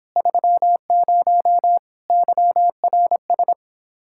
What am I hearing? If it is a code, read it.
Morse code, heard as a keyed tone, about 29 wpm, 700 Hz. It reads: 30 YRS